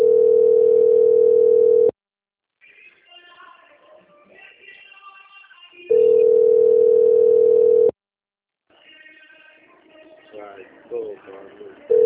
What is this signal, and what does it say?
Ringback tone (North America), 440 + 480 Hz, with speech.
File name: rg-1600-unknown- - - .37094